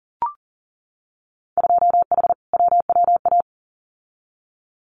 Morse code, heard as a keyed tone, about 41 wpm, 700 Hz. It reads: 2H WWA
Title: Winter Field Day Exchanges 40wpm